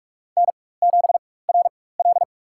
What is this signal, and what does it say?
Morse code, heard as a keyed tone, about 43 wpm, spaced as for 21 wpm, 700 Hz. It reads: N7RL
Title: USA Call Signs - 4x Character Spacing - 45wpm